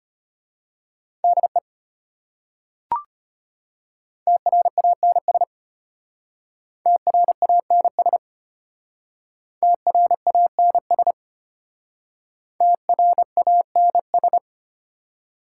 Morse code, heard as a keyed tone, about 32 wpm, 690 Hz. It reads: DE TRANS TRANS TRANS TRANS